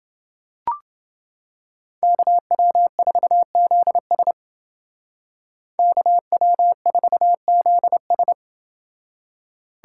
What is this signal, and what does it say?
Morse code, heard as a keyed tone, about 28 wpm, 700 Hz. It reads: KW4ZS KW4ZS